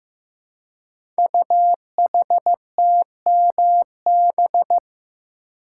Morse code, heard as a keyed tone, about 15 wpm, 700 Hz. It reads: UHTMB
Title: Code Groups - 5 Letter:Numbers - 15wpm